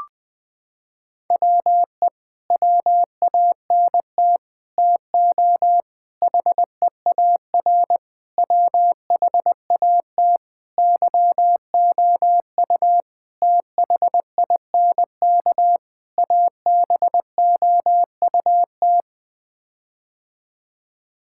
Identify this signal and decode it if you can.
Morse code, heard as a keyed tone, about 20 wpm, 700 Hz. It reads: WE WANT TO HEAR WHAT YOU THINK ABOUT
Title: Sets of 8 Words - 20wpm